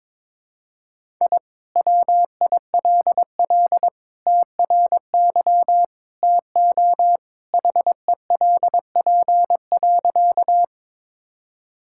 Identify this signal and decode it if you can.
Morse code, heard as a keyed tone, about 22 wpm, 700 Hz. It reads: I WILL TRY TO HELP.
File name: practice / 22wpm / Sentences with 3-Word Phrases - 22wpm